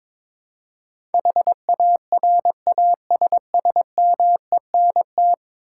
Morse code, heard as a keyed tone, about 22 wpm, 700 Hz. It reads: HARASSMENT
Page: morse-code-ninja-practice-22wpm